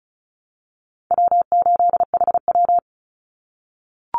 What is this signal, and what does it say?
Morse code, heard as a keyed tone, about 34 wpm, 700 Hz. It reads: W8HW
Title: Call Signs - 35wpm